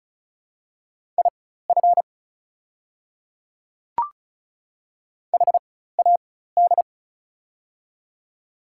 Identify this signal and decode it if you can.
Morse code, heard as a keyed tone, about 36 wpm, spaced as for 16 wpm, 690 Hz. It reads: IF HAD
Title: 4x Character Spacing - Top 100 Words - 35wpm